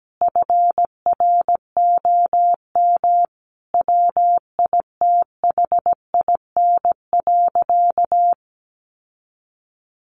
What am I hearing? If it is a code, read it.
Morse code, heard as a keyed tone, about 17 wpm, 700 Hz. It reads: FROM WITHIN.
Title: Sentences from Top 700 Words 17wpm